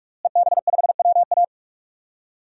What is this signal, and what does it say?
Morse code, heard as a keyed tone, about 41 wpm, 700 Hz. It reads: EB5WA
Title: DXCC-100 Call Signs - 5 Characters - Repeated 3x - 45wpm